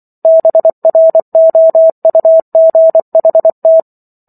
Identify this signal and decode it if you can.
Morse code, heard as a keyed tone, about 24 wpm, 650 Hz. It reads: BROUGHT
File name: Chapter 15 - 24 WPM